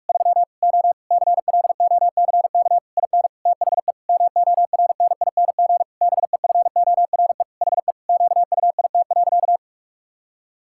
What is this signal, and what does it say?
Morse code, heard as a keyed tone, about 46 wpm, 700 Hz. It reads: 3 O CLOCK IN THE MORNING BEFORE HE QUIT.
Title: Pangram Challenge v2 - 45wpm